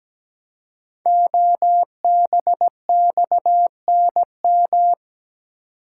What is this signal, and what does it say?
Morse code, heard as a keyed tone, about 17 wpm, 700 Hz. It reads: OBXNM
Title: Code Groups - 5 Letters - 17wpm